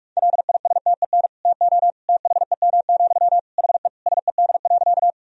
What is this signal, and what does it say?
Morse code, heard as a keyed tone, about 46 wpm, 700 Hz. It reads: LISTEN TO THEM, HE SED.
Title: Sentences from Top 400 Words and Abbreviations 45wpm